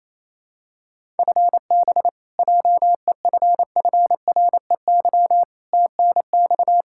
Morse code, heard as a keyed tone, about 28 wpm, 700 Hz. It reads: FB JEFFREY TNX
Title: Practice QSO - Exchange 3 - More Info - 28wpm